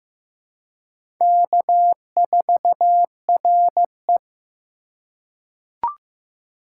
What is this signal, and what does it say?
Morse code, heard as a keyed tone, about 15 wpm, 700 Hz. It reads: K4RE
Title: USA Call Signs - 4 Characters - 2x Character Spacing - 15wpm